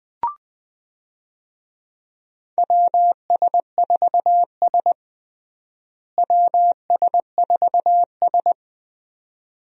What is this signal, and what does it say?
Morse code, heard as a keyed tone, about 21 wpm, 700 Hz. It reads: WS4S WS4S
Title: USA Call Signs - 4 Characters - Repeated 2x - 20wpm